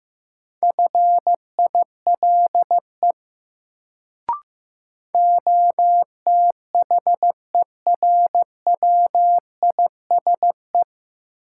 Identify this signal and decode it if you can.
Morse code, heard as a keyed tone, about 15 wpm, 700 Hz. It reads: FILE OTHERWISE
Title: Sentences from Top 1200 Words - New Word Drill Alt - 15wpm